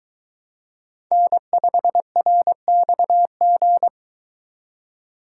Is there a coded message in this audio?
Yes: Morse code from 1.1 to 3.9 s, heard as a keyed tone, about 22 wpm, 700 Hz. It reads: N5RXG